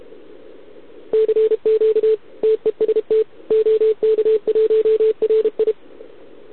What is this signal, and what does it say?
Morse code, heard as a keyed tone, about 32 wpm, 440 Hz. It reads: CQ TEST OK1RI
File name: ARCHIV / OKDX2020 / OK1RIb